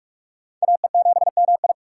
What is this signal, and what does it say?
Morse code, heard as a keyed tone, about 42 wpm, 700 Hz. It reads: AE7MI
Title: USA Call Signs - 5 Characters - 3x Character Spacing - 45wpm